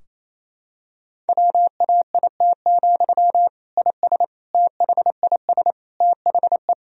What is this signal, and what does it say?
Morse code, heard as a keyed tone, about 28 wpm, 700 Hz. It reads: WAIT, IS THIS THE
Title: Sentences from Top 1000 Words - Review Part 4 - 28wpm